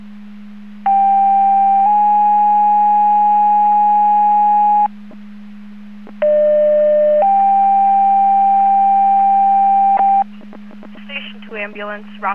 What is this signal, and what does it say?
Two-tone page, heard as a one-second tone, then a three-second tone, about 790 then 830 Hz, about 600 then 790 Hz, with speech. It is heard from 0.9 to 10.2 s.